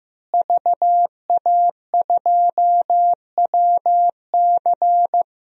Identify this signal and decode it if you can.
Morse code, heard as a keyed tone, about 15 wpm, 700 Hz. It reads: VA2WC
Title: DXCC-100 Call Signs - 5 Characters - Repeated 3x - 15wpm